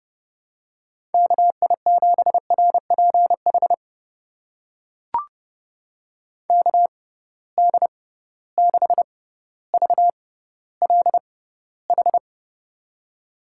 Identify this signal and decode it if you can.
Morse code, heard as a keyed tone, about 30 wpm, spaced as for 15 wpm, 700 Hz. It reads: KI7RPH KD6VLH
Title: USA Call Signs - 6 Characters - 6x Character Spacing - 30wpm